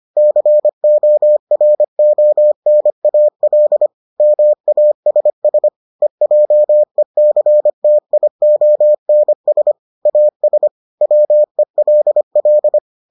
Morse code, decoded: CORONAL MASS EJECTIONS AS WELL

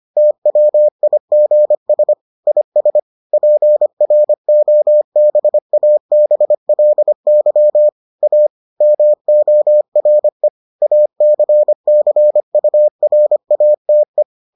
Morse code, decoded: TWIGS IS PROBABLY A MORE ACCURATE

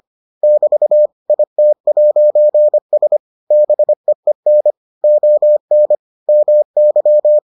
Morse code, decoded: <BT> IT'S BEEN ON MY

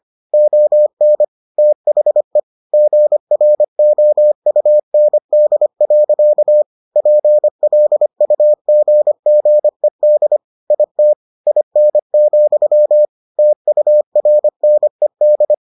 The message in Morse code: ON THE GROUND. PLUGGED IT IN, TURNED

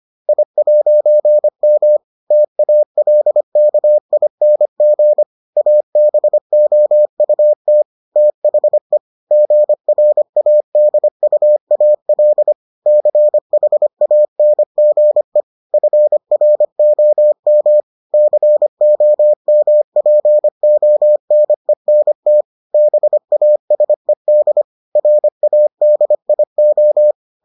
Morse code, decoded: I'M TALKING ABOUT THE GRADUAL CHANGE FROM COMPONENT BASED RADIO

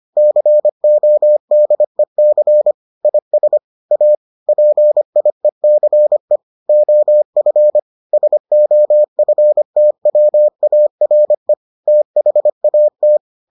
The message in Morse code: CODEC IS A PIECE OF SOFTWARE THAT

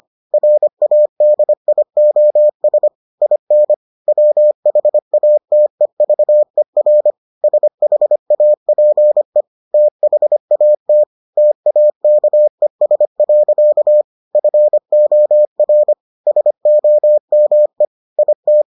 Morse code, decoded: RADIOS IN WHATEVER SHAPE THAT TAKES. FOR SOME IT

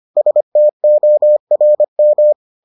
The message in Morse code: STORM

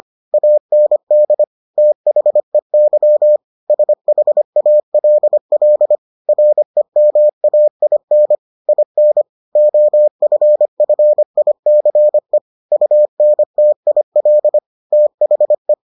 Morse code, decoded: AND THEY SHALL REMAIN IN OFFICE UNTIL THE

